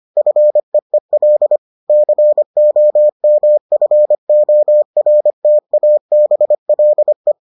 Morse code, heard as a keyed tone, about 25 wpm, 600 Hz. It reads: FEEL COMFORTABLE